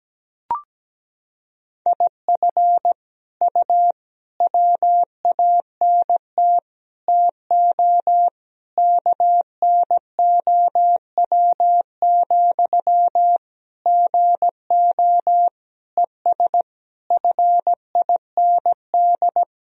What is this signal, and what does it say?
Morse code, heard as a keyed tone, about 17 wpm, 700 Hz. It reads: IF U WANT TO KNOW, GO ES FIND
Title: Sentences from Top 200 Words and Abbreviations 17wpm